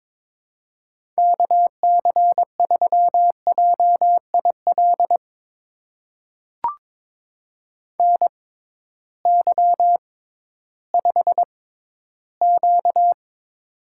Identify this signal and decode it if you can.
Morse code, heard as a keyed tone, about 22 wpm, spaced as for 13 wpm, 700 Hz. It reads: KC3JIL NY5Q
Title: USA Call Signs - 6x Character Spacing - 22wpm